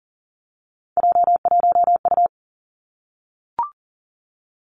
Morse code, heard as a keyed tone, about 41 wpm, 700 Hz. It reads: J1U
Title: Three Challenging Letter-Number Combinations 40wpm